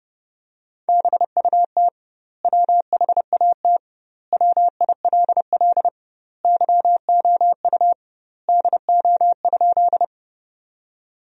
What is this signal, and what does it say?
Morse code, heard as a keyed tone, about 32 wpm, 700 Hz. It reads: BUT WHAT WILL YOU DO?
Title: 2x Word-Spacing - Sentences from Top 100 Words - 30wpm